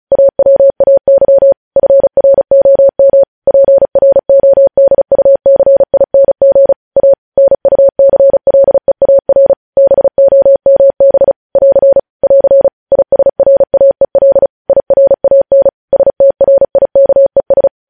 Morse code, heard as a keyed tone, about 36 wpm, 550 Hz. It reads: AWAY FROM PRODUCING A NUCLEAR BOMB <AR> <AR> ISRAEL IRAN STRIKES
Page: news headlines in morse code at 35 wpm